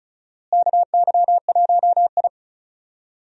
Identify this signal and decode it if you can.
Morse code, heard as a keyed tone, about 36 wpm, 700 Hz. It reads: KY1I
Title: USA Call Signs - 4 Characters - Repeated 2x - 35wpm